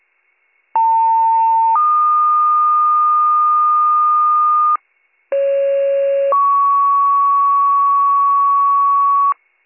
Two-tone page, heard as a one-second tone, then a three-second tone, about 900 then 1220 Hz, about 550 then 1120 Hz. It is heard from 0.7 to 9.3 s.